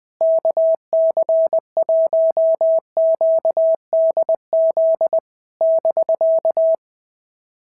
Morse code, heard as a keyed tone, about 19 wpm, 650 Hz. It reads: KC1QDZ <BK>
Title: POTA QSO Practice - 20wpm